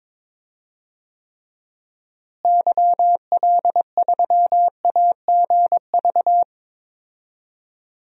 Morse code, heard as a keyed tone, about 21 wpm, 700 Hz. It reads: YL3AGV
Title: DXCC-100 Call Signs - 6 Characters - Repeated 2x - 22wpm